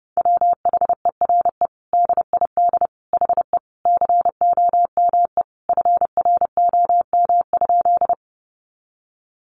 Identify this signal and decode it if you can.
Morse code, heard as a keyed tone, about 30 wpm, 700 Hz. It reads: WHERE DID HE COME FROM?
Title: Sentences from Top 200 Words 30wpm